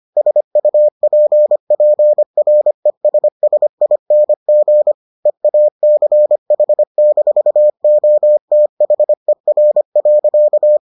Morse code, heard as a keyed tone, about 25 wpm, 600 Hz. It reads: SUPPRESSING EACH-OTHER.